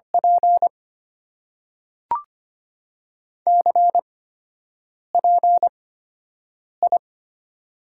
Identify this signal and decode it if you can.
Morse code, heard as a keyed tone, about 25 wpm, spaced as for 7 wpm, 700 Hz. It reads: P CPI
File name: Top 90 QSO Elements - 8x Character Spacing - 25wpm